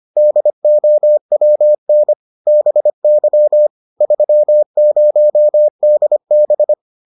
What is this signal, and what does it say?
Morse code, heard as a keyed tone, about 25 wpm, 600 Hz. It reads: DOWN BY 30DB